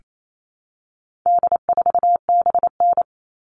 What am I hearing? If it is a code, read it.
Morse code, heard as a keyed tone, about 28 wpm, 700 Hz. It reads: D4BN
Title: All Cut Numbers - 4-Digit 28wpm